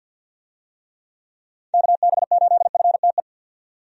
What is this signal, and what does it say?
Morse code, heard as a keyed tone, about 52 wpm, 700 Hz. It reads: KB8FTE
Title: USA Call Signs - 6 Characters - 50wpm